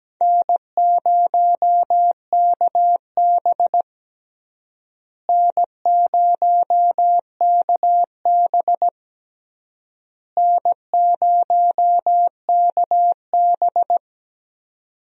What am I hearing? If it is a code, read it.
Morse code, heard as a keyed tone, about 17 wpm, 700 Hz. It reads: N0KB N0KB N0KB
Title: USA Call Signs - 4 Characters - Repeated 3x - 17wpm